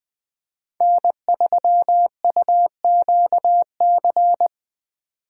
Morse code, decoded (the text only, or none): N3UQC